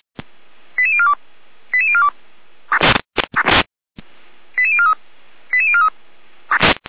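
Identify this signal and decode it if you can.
Five-tone selective call (ZVEI), heard as quick runs of five tones, tones of about 70 ms. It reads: 88042 88042 88043 88043
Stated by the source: ZVEI five-tone call received on an SDR, from the Feuerwehr Orlamünde (fire brigade) audio archive.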